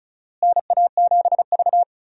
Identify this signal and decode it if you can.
Morse code, heard as a keyed tone, about 36 wpm, 700 Hz. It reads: NA7V
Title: USA Call Signs - 4 Characters - Repeated 2x - 35wpm